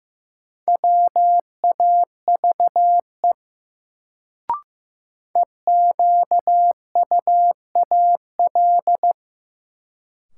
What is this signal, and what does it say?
Morse code, heard as a keyed tone, about 16 wpm, 700 Hz. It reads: WAVE EQUAL
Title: Sentences from Top 1400 Words - New Word Drill - 15wpm